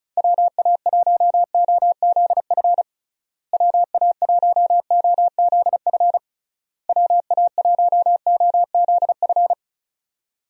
Morse code, heard as a keyed tone, about 36 wpm, 700 Hz. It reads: WA1OZF WA1OZF WA1OZF
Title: USA Call Signs - 6 Characters - Repeated 3x - 35wpm